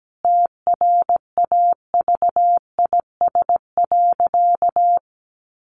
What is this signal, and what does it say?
Morse code, heard as a keyed tone, about 17 wpm, 700 Hz. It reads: TRAVIS.
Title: A Sound Of Thunder 17wpm